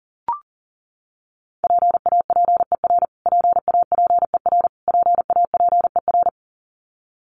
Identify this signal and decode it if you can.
Morse code, heard as a keyed tone, about 40 wpm, 700 Hz. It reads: PAPER PAPER PAPER
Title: Top 500 Words Repeated 40wpm